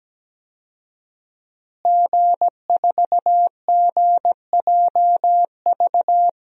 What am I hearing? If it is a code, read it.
Morse code, heard as a keyed tone, about 17 wpm, 700 Hz. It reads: G4GJV
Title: DXCC-100 Call Signs - 5 Characters - Repeated 3x - 17wpm